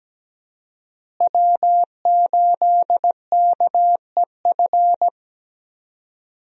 Morse code, heard as a keyed tone, about 17 wpm, 700 Hz. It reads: W8KEF
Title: USA Call Signs - 5 Characters - Repeated 2x - 17wpm